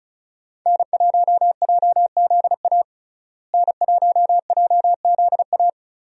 Morse code, heard as a keyed tone, about 36 wpm, 700 Hz. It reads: N1JZA N1JZA
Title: USA Call Signs - 5 Characters - Repeated 2x - 35wpm